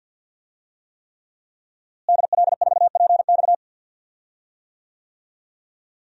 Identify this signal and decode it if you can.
Morse code, heard as a keyed tone, about 53 wpm, 700 Hz. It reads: DL4PX
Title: DXCC-100 Call Signs - 5 Characters - Repeated 3x - 50wpm